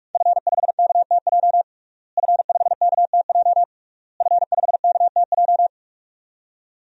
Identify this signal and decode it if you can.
Morse code, heard as a keyed tone, about 42 wpm, 700 Hz. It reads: F5KTJ F5KTJ F5KTJ
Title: DXCC-100 Call Signs - 5 Characters - Repeated 3x - 45wpm